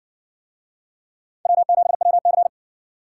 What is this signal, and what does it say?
Morse code, heard as a keyed tone, about 57 wpm, 700 Hz. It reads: W7WC